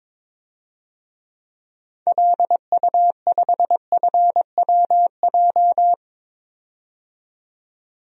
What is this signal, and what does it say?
Morse code, heard as a keyed tone, about 22 wpm, 700 Hz. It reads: LU5FWJ
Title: DXCC-100 Call Signs - 6 Characters - 22wpm